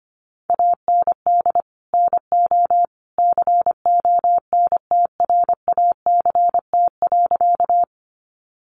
Morse code, heard as a keyed tone, about 25 wpm, 700 Hz. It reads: AND NO CONTRACT.
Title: Sentences from Top 1400 Words 25wpm